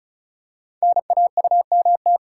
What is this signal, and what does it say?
Morse code, heard as a keyed tone, about 36 wpm, 690 Hz. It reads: NAUMT
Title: Code Groups - 5 Letters - 35wpm